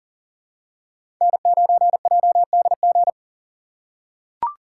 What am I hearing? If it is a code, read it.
Morse code, heard as a keyed tone, about 40 wpm, 700 Hz. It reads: N9JDG
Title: USA Call Signs - 3x Character Spacing - 40wpm